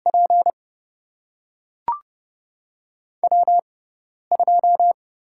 Morse code, heard as a keyed tone, about 30 wpm, spaced as for 10 wpm, 700 Hz. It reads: P W2
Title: USA Call Signs - 5 Characters - 6x Character Spacing - 30wpm